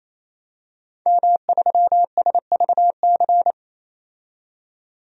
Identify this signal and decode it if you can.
Morse code, heard as a keyed tone, about 28 wpm, 700 Hz. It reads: M3SVC
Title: DXCC-100 Call Signs - 5 Characters - 28wpm